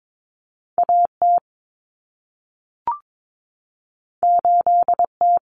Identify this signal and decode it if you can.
Morse code, heard as a keyed tone, about 22 wpm, 700 Hz. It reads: AT 8T
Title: Common Cut Numbers - 2-Digit 22wpm